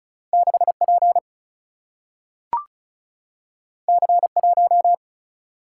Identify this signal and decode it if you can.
Morse code, heard as a keyed tone, about 36 wpm, 700 Hz. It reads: 6P C1
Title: Single-Letter Single-Number 35wpm